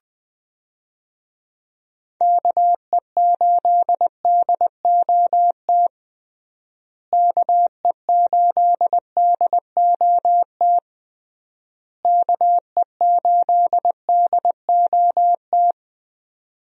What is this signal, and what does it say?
Morse code, heard as a keyed tone, about 20 wpm, 700 Hz. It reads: KE8DOT KE8DOT KE8DOT
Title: USA Call Signs - 6 Characters - Repeated 3x - 20wpm